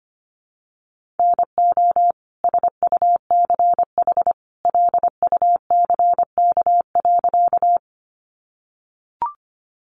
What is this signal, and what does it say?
Morse code, heard as a keyed tone, about 25 wpm, 700 Hz. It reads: NO SUCH LUCK.
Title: Sentences from Top 1500 Words 25wpm